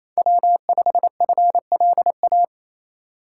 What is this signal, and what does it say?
Morse code, heard as a keyed tone, about 34 wpm, 700 Hz. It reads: W5FLA
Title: Speed Racing - USA 1x3 Call Signs 28wpm